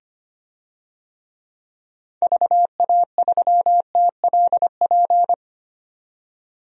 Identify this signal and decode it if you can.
Morse code, heard as a keyed tone, about 25 wpm, 700 Hz. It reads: VA3TLP
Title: DXCC-100 Call Signs - 6 Characters - 25wpm